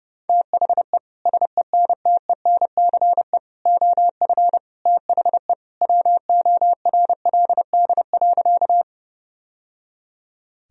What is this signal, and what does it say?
Morse code, heard as a keyed tone, about 30 wpm, 700 Hz. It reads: THE SENTENCE OF THE WORLD.
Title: Sentences from Top 1000 Words - Review Part 1 - 30wpm